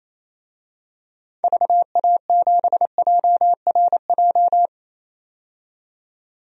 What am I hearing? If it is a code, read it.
Morse code, heard as a keyed tone, about 28 wpm, 700 Hz. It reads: VA7JRJ
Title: DXCC-100 Call Signs - 6 Characters - 28wpm